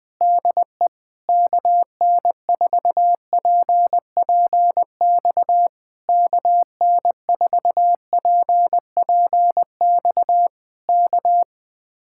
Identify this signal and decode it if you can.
Morse code, heard as a keyed tone, about 20 wpm, 700 Hz. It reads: DE KN4PPX KN4PPX K